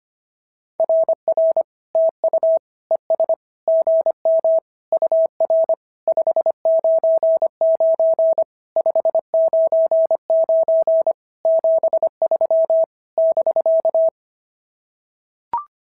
Morse code, heard as a keyed tone, about 25 wpm, 690 Hz. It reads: RR TU ES GM UR 599 599 73 <BK>
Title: SOTA QSO Practice 25wpm